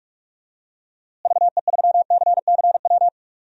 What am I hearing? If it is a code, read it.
Morse code, heard as a keyed tone, about 46 wpm, 700 Hz. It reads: VE3CCW
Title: DXCC-100 Call Signs - 6 Characters - Repeated 3x - 45wpm